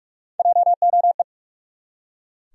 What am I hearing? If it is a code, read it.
Morse code, heard as a keyed tone, about 41 wpm, 700 Hz. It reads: JOE
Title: USA Common Names - Max 3 Letters - 45wpm